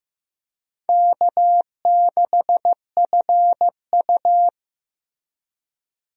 Morse code, heard as a keyed tone, about 15 wpm, 700 Hz. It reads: K6FU
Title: USA Call Signs - 4 Characters - Repeated 3x - 15wpm